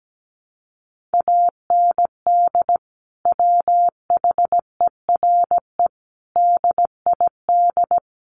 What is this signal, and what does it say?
Morse code, heard as a keyed tone, about 17 wpm, 700 Hz. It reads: AND WHERE DID
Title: Sentences from Top 1500 Words 17wpm